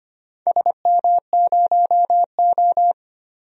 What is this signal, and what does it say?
Morse code, heard as a keyed tone, about 25 wpm, 700 Hz. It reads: SM0O
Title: DXCC-100 Call Signs - 4 Characters - 25wpm